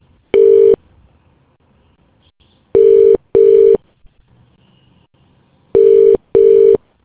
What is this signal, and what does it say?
Ringback tone (UK), 400 + 450 Hz.